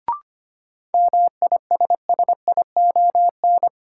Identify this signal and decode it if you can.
Morse code, heard as a keyed tone, about 25 wpm, 700 Hz. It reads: MISSION
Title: Sentences from Top 1600 Words - New Word Drill - 25wpm